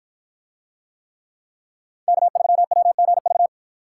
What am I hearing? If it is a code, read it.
Morse code, heard as a keyed tone, about 50 wpm, 700 Hz. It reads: K3WGV